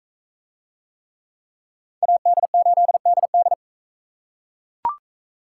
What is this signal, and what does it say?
Morse code, heard as a keyed tone, about 40 wpm, 700 Hz. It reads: AD8DD